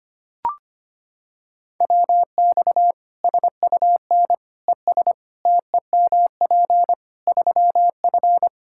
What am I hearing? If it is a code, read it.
Morse code, heard as a keyed tone, about 25 wpm, 700 Hz. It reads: WX SUN ES TEMP 3F
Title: Practice QSO - Exchange 2 - Info - 25wpm